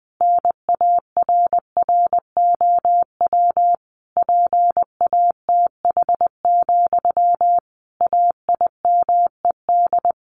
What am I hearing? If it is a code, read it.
Morse code, heard as a keyed tone, about 20 wpm, 700 Hz. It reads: NARROW PATH, AIMED
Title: A Sound Of Thunder 20wpm